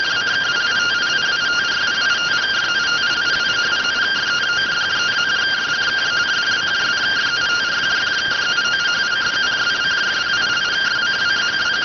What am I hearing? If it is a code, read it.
Two-tone teleprinter signal, about 170 Hz shift.